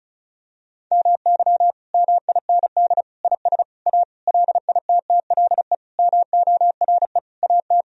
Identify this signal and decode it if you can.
Morse code, heard as a keyed tone, about 36 wpm, 700 Hz. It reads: MY MIND IS A LITTLE MORE AT
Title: Sentences from Top 1000 Words - Review Part 1 - 35wpm